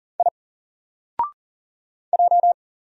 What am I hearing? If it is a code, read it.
Morse code, heard as a keyed tone, about 40 wpm, 700 Hz. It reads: I J
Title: Single Character (Letter:Number:Punctuation:Prosign) Alt 40wpm